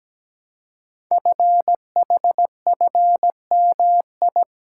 Morse code, heard as a keyed tone, about 17 wpm, 700 Hz. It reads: FHFMI
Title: Code Groups - 5 Letter:Numbers - 17wpm